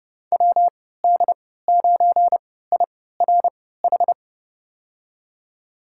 Morse code, heard as a keyed tone, about 31 wpm, spaced as for 17 wpm, 700 Hz. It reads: WD9IRH
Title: USA Call Signs - 6 Characters - 3x Character Spacing - 30wpm